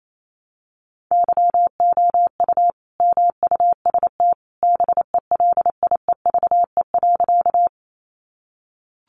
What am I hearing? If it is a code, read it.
Morse code, heard as a keyed tone, about 28 wpm, 700 Hz. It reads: YOU MUST BELIEVE.
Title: Sentences from Top 1000 Words 28wpm